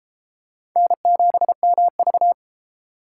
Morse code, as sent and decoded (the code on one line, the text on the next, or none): -. --... -- ...-
N7MV